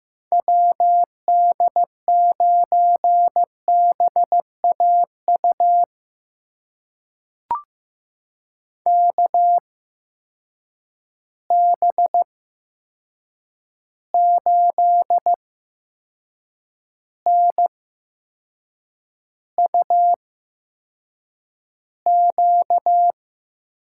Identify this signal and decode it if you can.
Morse code, heard as a keyed tone, about 15 wpm, spaced as for 6 wpm, 700 Hz. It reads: WD9BAU KB8NUQ
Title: USA Call Signs - 6 Characters - 8x Character Spacing - 15wpm